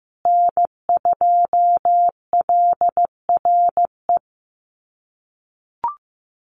Morse code, heard as a keyed tone, about 15 wpm, 700 Hz. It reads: N2LRE
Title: USA Call Signs Repeated 3x 15wpm